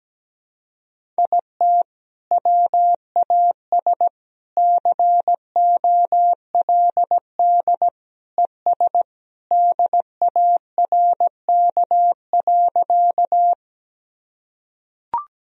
Morse code, heard as a keyed tone, about 17 wpm, 700 Hz. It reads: IT WAS COLD ES DARK.
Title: Sentences from Top 500 Words and Abbreviations 17wpm